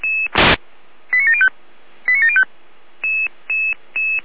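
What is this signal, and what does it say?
Five-tone selective call (ZVEI), heard as quick runs of five tones, tones of about 70 ms. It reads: 89705 89705
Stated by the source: Five-tone ZVEI selective call received on an SDR, channel or frequency 455.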